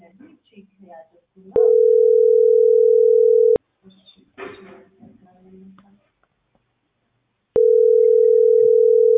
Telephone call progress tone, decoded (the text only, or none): ringback tone